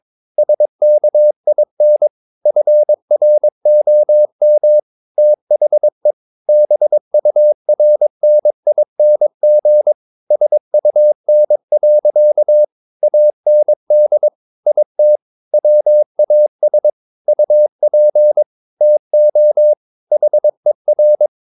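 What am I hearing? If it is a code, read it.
Morse code, heard as a keyed tone, about 22 wpm, 600 Hz. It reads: SKIN FROM THE BURNING SUN. AND IT WAS UP TO HER
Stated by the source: Morse code practice at 22 wpm, sending text from an English novel.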